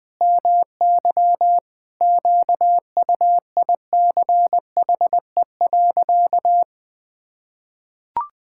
Morse code, heard as a keyed tone, about 20 wpm, 700 Hz. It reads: MY QUICHE.
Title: Pangram Challenge v2 - 20wpm